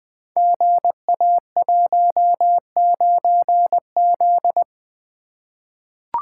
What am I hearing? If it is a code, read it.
Morse code, heard as a keyed tone, about 20 wpm, 700 Hz. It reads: GA19Z
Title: Code Groups - 5 Letter:Numbers - 20wpm